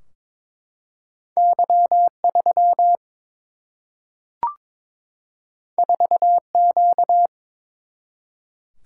Morse code, heard as a keyed tone, about 22 wpm, 700 Hz. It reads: Y 3 4 Q
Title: Single-Letter Single-Number 22wpm